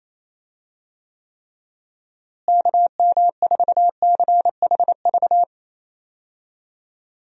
Morse code, heard as a keyed tone, about 28 wpm, 700 Hz. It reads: KM4CHV
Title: USA Call Signs - 6 Characters - Repeated 2x - 28wpm